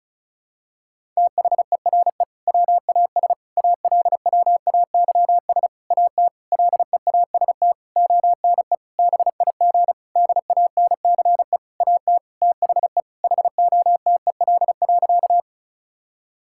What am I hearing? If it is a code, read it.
Morse code, heard as a keyed tone, about 34 wpm, 700 Hz. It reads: THERE WAS ALWAYS AT LEAST ONE BIG DANCE AT THE HOTEL.
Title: Sentences from Top 1600 Words 35wpm